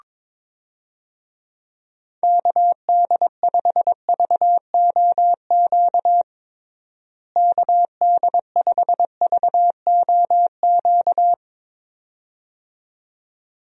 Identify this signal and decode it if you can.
Morse code, heard as a keyed tone, about 22 wpm, 700 Hz. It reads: KD5VOQ KD5VOQ
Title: USA Call Signs - 6 Characters - Repeated 2x - 22wpm